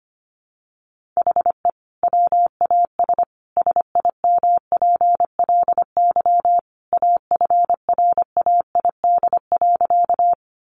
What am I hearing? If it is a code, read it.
Morse code, heard as a keyed tone, about 25 wpm, 700 Hz. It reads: HE WAS SIMPLY AFRAID.
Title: Sentences from Top 1500 Words 25wpm